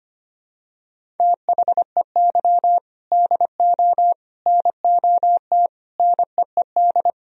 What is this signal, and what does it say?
Morse code, heard as a keyed tone, about 25 wpm, 700 Hz. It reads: THEY DO NOT NEED
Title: Sentences from Top 1000 Words - Review Part 1 - 25wpm